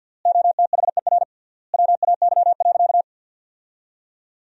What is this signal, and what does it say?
Morse code, heard as a keyed tone, about 50 wpm, 700 Hz. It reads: OTHER WAY.
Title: 3x Word-Spacing - Sentences from Top 100 Words - 50wpm